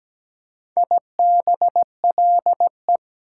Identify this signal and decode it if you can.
Morse code, heard as a keyed tone, about 17 wpm, 700 Hz. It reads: IBLE
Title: Prefixes and Suffixes 17wpm